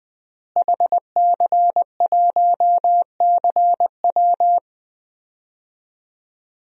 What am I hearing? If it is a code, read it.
Morse code, heard as a keyed tone, about 21 wpm, 700 Hz. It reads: HC1CW
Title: DXCC-100 Call Signs - 5 Characters - Repeated 3x - 20wpm